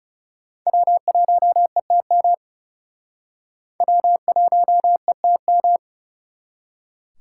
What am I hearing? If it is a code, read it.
Morse code, heard as a keyed tone, about 38 wpm, 700 Hz. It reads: W1ETM W1ETM
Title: Speed Racing - USA 1x3 Call Signs 30wpm